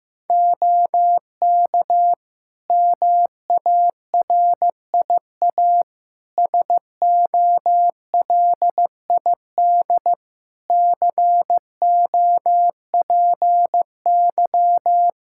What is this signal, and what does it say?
Morse code, heard as a keyed tone, about 16 wpm, 700 Hz. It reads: OK MARIA SOLID COPY
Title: Practice QSO - Exchange 3 - More Info - 15wpm